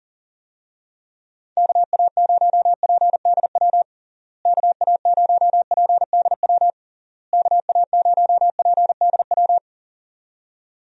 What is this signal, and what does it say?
Morse code, heard as a keyed tone, about 40 wpm, 700 Hz. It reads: KA0PDW KA0PDW KA0PDW
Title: USA Call Signs - 6 Characters - Repeated 3x - 40wpm